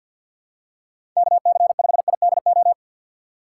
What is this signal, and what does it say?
Morse code, heard as a keyed tone, about 50 wpm, 700 Hz. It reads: KC5IDO
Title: USA Call Signs - 6 Characters - 4x Character Spacing - 50wpm